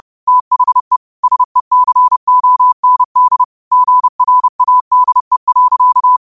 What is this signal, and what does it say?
Morse code, heard as a keyed tone, about 30 wpm, 1000 Hz. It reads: THE SECOND GRADE.
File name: qotdfiles / 30wpm